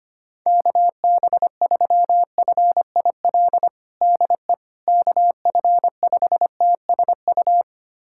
Morse code, heard as a keyed tone, about 25 wpm, 700 Hz. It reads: KB3FIL DE KF5TSU
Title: USA Callsign DE Callsign - 6 Characters - 25wpm